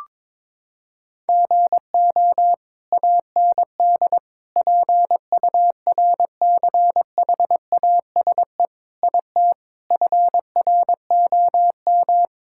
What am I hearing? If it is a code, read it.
Morse code, heard as a keyed tone, about 22 wpm, 700 Hz. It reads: GO AND PURCHASE IT FROM
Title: Sentences from Top 1000 Words - Review Part 5 - 22wpm